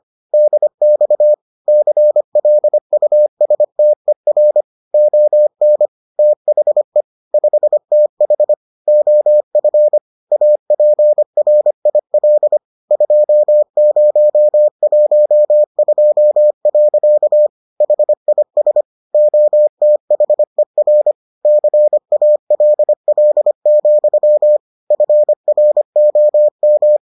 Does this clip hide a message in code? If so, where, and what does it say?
Yes: Morse code from 0.3 to 27.0 s, heard as a keyed tone, about 25 wpm, 600 Hz. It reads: DX CLUSTER ON THE 5TH OF APRIL 2012. HIS OTHER CALL, FROM